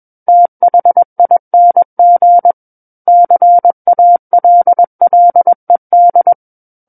Morse code, heard as a keyed tone, about 21 wpm, 700 Hz. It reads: THING CALLED
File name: Book One / 12-Chapter 12 - 21 WPM